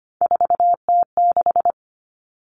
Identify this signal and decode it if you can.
Morse code, heard as a keyed tone, about 25 wpm, 700 Hz. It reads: 4T6
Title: All Cut Numbers - 3-Digit 25wpm